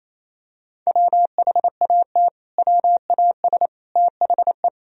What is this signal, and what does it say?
Morse code, heard as a keyed tone, about 28 wpm, 700 Hz. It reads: WHAT WAS THE
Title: Sentences from Top 1000 Words - Review Part 3 - 28wpm